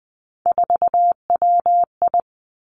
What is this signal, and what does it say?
Morse code, heard as a keyed tone, about 20 wpm, 700 Hz. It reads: 4WI
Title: Three Challenging Letter-Number Combinations 20wpm